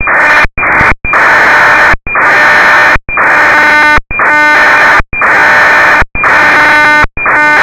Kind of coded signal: ACARS aircraft data